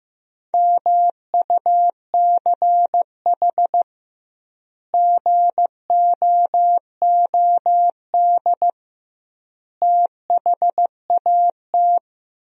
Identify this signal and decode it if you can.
Morse code, heard as a keyed tone, about 16 wpm, 700 Hz. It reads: MUCH GOOD THAT
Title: 2x Word-Spacing - Sentences from Top 200 Words - 15wpm